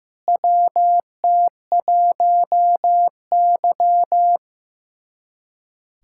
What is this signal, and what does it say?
Morse code, heard as a keyed tone, about 15 wpm, 700 Hz. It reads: WT1Y
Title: USA Callsign DE Callsign 15wpm